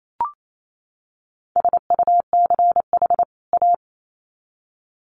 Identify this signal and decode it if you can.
Morse code, heard as a keyed tone, about 28 wpm, 700 Hz. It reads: SUCH A
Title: Sets of 2 Words 28wpm